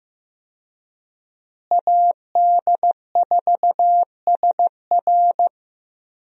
Morse code, heard as a keyed tone, about 15 wpm, 700 Hz. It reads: AD4SR
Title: USA Call Signs - 5 Characters - Repeated 2x - 15wpm